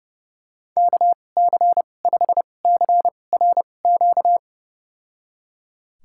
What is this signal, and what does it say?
Morse code, heard as a keyed tone, about 32 wpm, spaced as for 22 wpm, 700 Hz. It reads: KC5CRQ
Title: USA Call Signs - 6 Characters - 2x Character Spacing - 30wpm